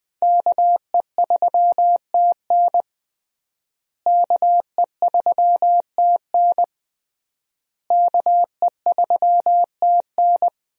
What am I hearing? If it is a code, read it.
Morse code, heard as a keyed tone, about 21 wpm, 700 Hz. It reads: KE3TN KE3TN KE3TN
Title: USA Call Signs - 5 Characters - Repeated 3x - 20wpm